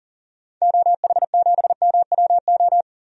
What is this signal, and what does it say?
Morse code, heard as a keyed tone, about 40 wpm, 700 Hz. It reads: OH7MWO
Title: DXCC-100 Call Signs - 6 Characters - 40wpm